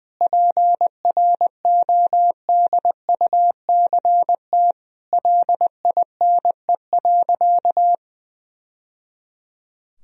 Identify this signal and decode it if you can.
Morse code, heard as a keyed tone, about 21 wpm, 700 Hz. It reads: PRODUCT LINE.
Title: Sentences from Top 1000 Words - Review Part 3 - 20wpm